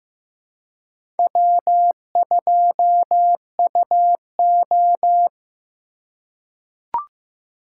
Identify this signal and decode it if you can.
Morse code, heard as a keyed tone, about 15 wpm, 700 Hz. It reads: W2UO
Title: USA Call Signs - 4x Character Spacing - 15wpm